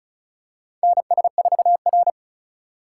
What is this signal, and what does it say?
Morse code, heard as a keyed tone, about 36 wpm, 700 Hz. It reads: NS4R
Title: USA Call Signs - 4 Characters - 2x Character Spacing - 35wpm